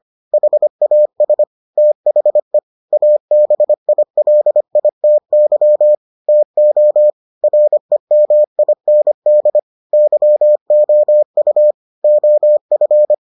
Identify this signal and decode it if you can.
Morse code, heard as a keyed tone, about 25 wpm, 600 Hz. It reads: HAS THE ABILITY TO REMIND YOU OF